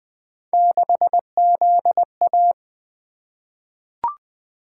Morse code, heard as a keyed tone, about 20 wpm, 700 Hz. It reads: 6ZA
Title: Three Letter-Number Combinations - 20wpm